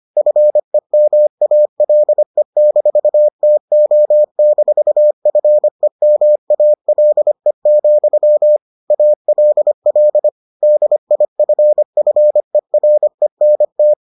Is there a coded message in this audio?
Yes: Morse code from 0.2 to 13.9 s, heard as a keyed tone, about 25 wpm, 600 Hz. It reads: FEMALE-TO-FEMALE, ALL DIFFERENT